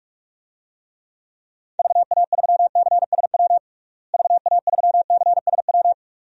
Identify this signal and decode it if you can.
Morse code, heard as a keyed tone, about 41 wpm, 700 Hz. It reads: VA3CSW VA3CSW
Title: DXCC-100 Call Signs - 6 Characters - Repeated 2x - 45wpm